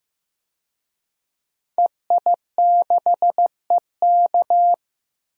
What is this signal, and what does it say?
Morse code, heard as a keyed tone, about 15 wpm, 700 Hz. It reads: EI6EK
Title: DXCC-100 Call Signs - 5 Characters - 15wpm